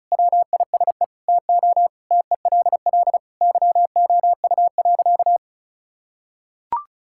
Morse code, heard as a keyed tone, about 34 wpm, 700 Hz. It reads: WISE TO TELL YOU.
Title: Sentences from Top 1600 Words 35wpm